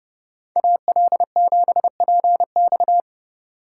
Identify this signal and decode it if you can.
Morse code, heard as a keyed tone, about 29 wpm, 700 Hz. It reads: AL7PX
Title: USA Call Signs - 3x Character Spacing - 30wpm